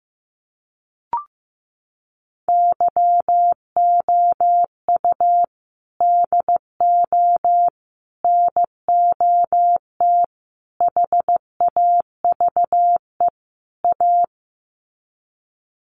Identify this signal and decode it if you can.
Morse code, heard as a keyed tone, about 15 wpm, 700 Hz. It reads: YOU DO NOT HAVE A
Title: Sets of 5 Words 15wpm